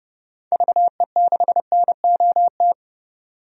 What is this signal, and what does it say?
Morse code, heard as a keyed tone, about 30 wpm, 700 Hz. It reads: VE6NOT
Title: DXCC-100 Call Signs - 6 Characters - 30wpm